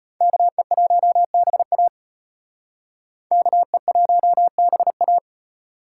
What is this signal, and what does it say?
Morse code, heard as a keyed tone, about 32 wpm, 700 Hz. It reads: KE1BA KE1BA